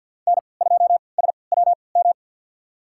Morse code, heard as a keyed tone, about 46 wpm, spaced as for 27 wpm, 700 Hz. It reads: N2SWM
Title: USA Call Signs - 3x Character Spacing - 50wpm